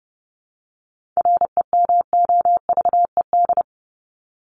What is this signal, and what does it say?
Morse code, heard as a keyed tone, about 30 wpm, 700 Hz. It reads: REMOVED
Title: ED Suffix - Top 100 Words with Base Words 30wpm